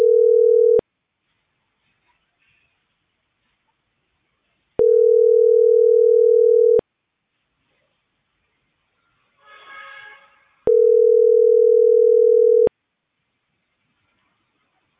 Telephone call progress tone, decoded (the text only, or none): ringback tone